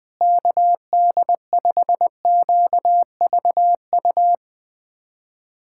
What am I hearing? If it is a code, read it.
Morse code, heard as a keyed tone, about 20 wpm, 700 Hz. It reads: KD5QVU
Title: USA Callsign DE Callsign - 6 Characters - 20wpm